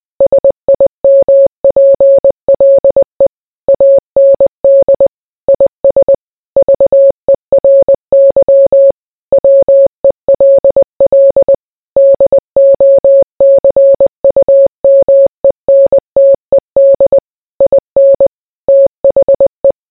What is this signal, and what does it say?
Morse code, heard as a keyed tone, about 20 wpm, 550 Hz. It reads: SIMPLE AND IS VERY WELL DOCUMENTED IN THE